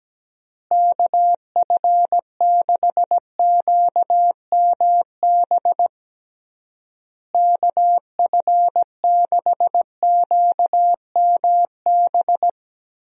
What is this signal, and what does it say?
Morse code, heard as a keyed tone, about 17 wpm, 700 Hz. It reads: KF6QMB KF6QMB
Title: USA Call Signs - 6 Characters - Repeated 3x - 17wpm